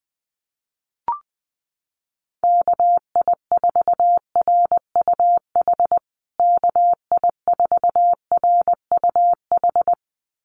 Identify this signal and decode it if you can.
Morse code, heard as a keyed tone, about 20 wpm, 700 Hz. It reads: KI4RUH KI4RUH
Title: USA Call Signs Repeated 2x 20wpm